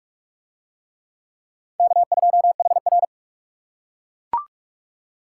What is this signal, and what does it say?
Morse code, heard as a keyed tone, about 39 wpm, 700 Hz. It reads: K2HR